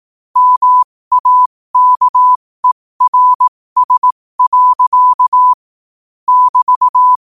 Morse code, heard as a keyed tone, about 18 wpm, spaced as for 15 wpm, 1000 Hz. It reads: MAKERS. <BT>
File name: qotdfiles / 15wpm